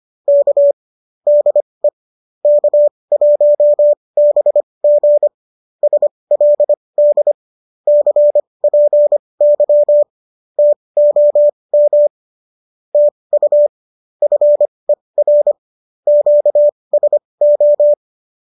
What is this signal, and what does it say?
Morse code, heard as a keyed tone, about 25 wpm, spaced as for 20 wpm, 590 Hz. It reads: K DE K1BG SLD CPY TOM TU FER QSO